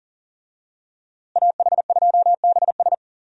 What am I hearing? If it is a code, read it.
Morse code, heard as a keyed tone, about 37 wpm, 700 Hz. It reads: AH2BS
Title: USA Call Signs - 5 Characters - Repeated 3x - 40wpm